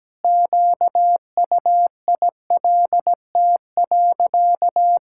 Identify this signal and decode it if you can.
Morse code, heard as a keyed tone, about 17 wpm, 700 Hz. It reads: QUILT.
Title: Pangram Challenge v2 - 17wpm